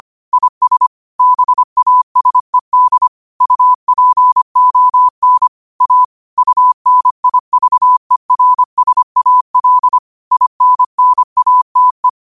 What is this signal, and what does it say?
Morse code, heard as a keyed tone, about 25 wpm, 1000 Hz. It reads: IS BASED UPON A UNIVERSAL INNATE
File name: qotdfiles / 25wpm